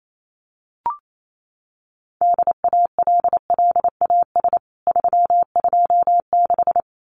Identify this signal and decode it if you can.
Morse code, heard as a keyed tone, about 28 wpm, 700 Hz. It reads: DALLAS 326
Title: CWT Exchange 28wpm